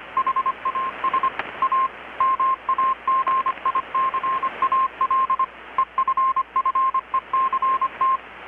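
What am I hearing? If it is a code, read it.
Morse code, heard as a keyed tone, about 25 wpm, 1060 Hz, with radio noise. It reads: HAS A MAGICAL EFFECT
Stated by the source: Morse from KPH's Night of Nights XX maritime radiotelegraphy event, received in Chiapas, Mexico about 3,800 km away on a Kenwood R-600 with a grounded 10 m long-wire antenna.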